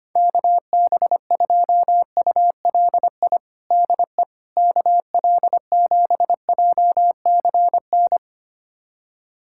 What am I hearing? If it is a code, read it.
Morse code, heard as a keyed tone, about 25 wpm, 700 Hz. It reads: KB2ULI DE KL7JCN
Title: USA Callsign DE Callsign 25wpm